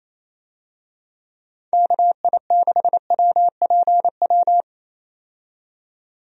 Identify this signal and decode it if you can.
Morse code, heard as a keyed tone, about 27 wpm, 700 Hz. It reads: KI6WPW
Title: USA Call Signs - 6 Characters - 28wpm